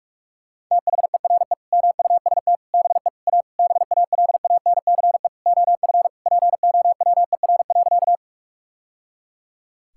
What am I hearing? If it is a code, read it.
Morse code, heard as a keyed tone, about 41 wpm, 700 Hz. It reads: THERE MUST BE A BALANCE OF POWER.
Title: Sentences from Top 1000 Words - Review Part 6 - 45wpm